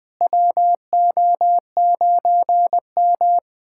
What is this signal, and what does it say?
Morse code, heard as a keyed tone, about 20 wpm, 700 Hz. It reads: WO9M
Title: USA Callsign DE Callsign - 4 Characters - 20wpm